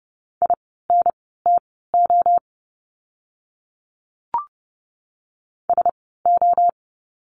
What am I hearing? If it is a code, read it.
Morse code, heard as a keyed tone, about 30 wpm, spaced as for 17 wpm, 700 Hz. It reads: INTO SO
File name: course / 30:10wpm / 230 Lesson 37f - Letter x - Words 30wpm